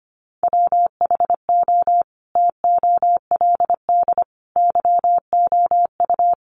Morse code, heard as a keyed tone, about 25 wpm, 700 Hz. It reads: WHO TOLD YOU
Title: Sentences from Top 400 Words 25wpm